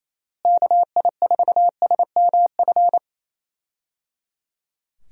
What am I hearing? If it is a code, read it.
Morse code, heard as a keyed tone, about 28 wpm, 700 Hz. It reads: KI4SMF
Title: USA Call Signs - 6 Characters - Repeated 3x - 28wpm